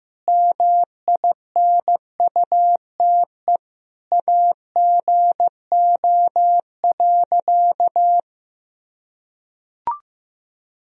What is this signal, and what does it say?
Morse code, heard as a keyed tone, about 16 wpm, 700 Hz. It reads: MINUTE AGO.
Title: Sentences from Top 500 Words and Abbreviations 15wpm